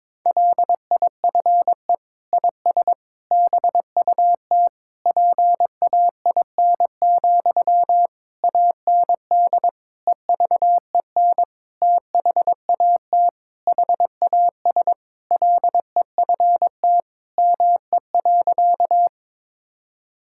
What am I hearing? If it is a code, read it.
Morse code, heard as a keyed tone, about 22 wpm, 700 Hz. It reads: LIFE IS BUT PAIN, AND EVEN THAT HAS LEFT ME.
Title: Sentences from Top 1600 Words 22wpm